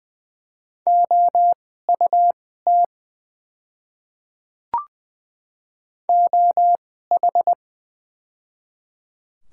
Morse code, heard as a keyed tone, about 20 wpm, spaced as for 14 wpm, 700 Hz. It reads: OUT OH
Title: 2x Character Spacing - Top 500 Words - 20wpm